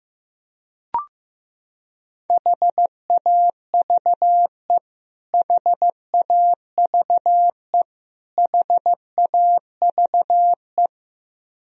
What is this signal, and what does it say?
Morse code, heard as a keyed tone, about 15 wpm, 700 Hz. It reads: HAVE HAVE HAVE
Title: Top 100 Words Repeated 15wpm